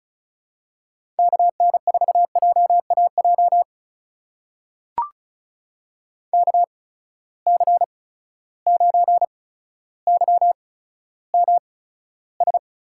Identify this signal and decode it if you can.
Morse code, heard as a keyed tone, about 36 wpm, spaced as for 15 wpm, 700 Hz. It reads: KN4JAJ KC9YMS
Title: USA Call Signs - 6 Characters - 8x Character Spacing - 35wpm